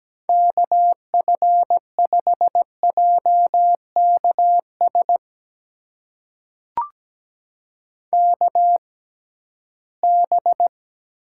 Morse code, heard as a keyed tone, about 17 wpm, spaced as for 13 wpm, 700 Hz. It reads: KF5JKS KB
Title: USA Call Signs - 6 Characters - 6x Character Spacing - 17wpm